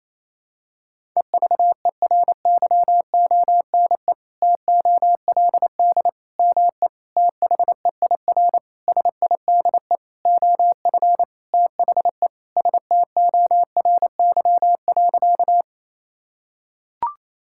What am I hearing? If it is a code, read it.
Morse code, heard as a keyed tone, about 28 wpm, 700 Hz. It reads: EVERYONE TOLD ME THEIR SIDE OF THE STORY.
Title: Sentences from Top 1000 Words - Review Part 5 - 28wpm